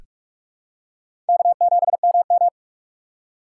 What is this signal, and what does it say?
Morse code, heard as a keyed tone, about 46 wpm, 700 Hz. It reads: K7MM
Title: USA Call Signs - 4 Characters - 4x Character Spacing - 45wpm